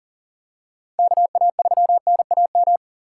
Morse code, heard as a keyed tone, about 37 wpm, 700 Hz. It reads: KA3NAM
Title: K1USN SST QSO Practice - 40wpm